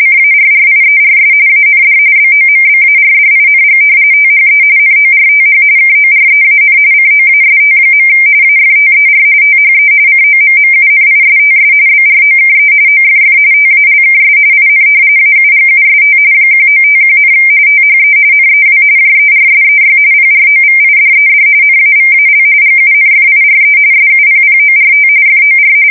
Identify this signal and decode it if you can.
RTTY, about 45.45 baud. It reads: CARRIERS AND BOMBARDING THE NORTH KOREAN COAST. AFTER THE COMMUNISTS BROKE OFF TALKS IN AUGUST 1951, THE NEW JERSEY PROVIDED NAVAL GUNFIRE SUPPORT FOR
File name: fsk170 / battleshipnj